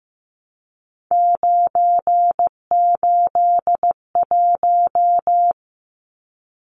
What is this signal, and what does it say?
Morse code, heard as a keyed tone, about 15 wpm, 700 Hz. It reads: 981
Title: Numbers - 3-Digit 15wpm